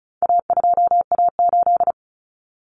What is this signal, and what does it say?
Morse code, heard as a keyed tone, about 34 wpm, 700 Hz. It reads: A2A8
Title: Common Cut Numbers - 4-Digit 35wpm